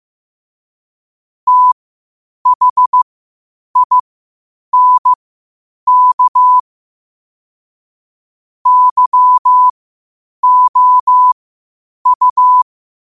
Morse code, heard as a keyed tone, about 15 wpm, spaced as for 8 wpm, 1000 Hz. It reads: THINK YOU
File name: qotdfiles / 8wpm